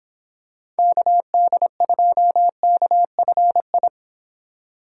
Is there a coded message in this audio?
Yes: Morse code from 0.8 to 3.9 s, heard as a keyed tone, about 25 wpm, 700 Hz. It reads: KD2KFI